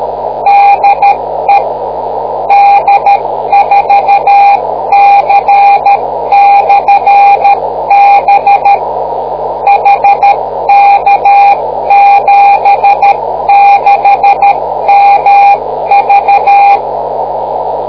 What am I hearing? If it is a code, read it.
Morse code, heard as a keyed tone, about 13 wpm, 800 Hz, with noise. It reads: DE D4C/B HK76MV